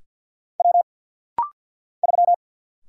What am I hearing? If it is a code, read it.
Morse code, heard as a keyed tone, about 50 wpm, 700 Hz. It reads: W 3
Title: Single Letter-Number Alt - Rapid-Fire 50wpm